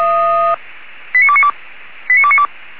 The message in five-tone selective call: 89292 89292

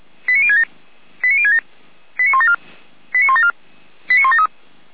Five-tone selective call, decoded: 89067 89067 89174 89174 89173

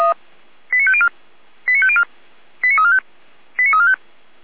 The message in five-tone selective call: 89504 89504 89345 89345

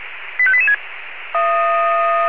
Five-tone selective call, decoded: 85886

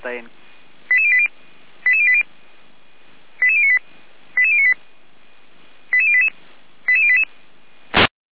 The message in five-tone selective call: 88080 88080 88098 88098 88088 88088

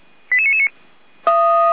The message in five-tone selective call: 88999

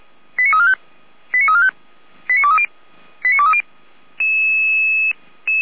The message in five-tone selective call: 89345 89345 89230 89230

